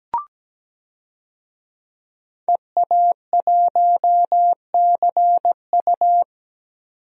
Morse code, heard as a keyed tone, about 17 wpm, 700 Hz. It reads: EA1CU
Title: DXCC-100 Call Signs - 5 Characters - 17wpm